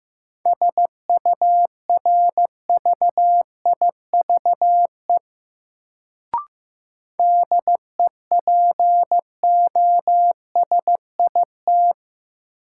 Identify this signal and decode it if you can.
Morse code, heard as a keyed tone, about 15 wpm, 700 Hz. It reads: SURVIVE DEPOSIT
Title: Sentences from Top 1300 Words - New Word Drill Alt - 15wpm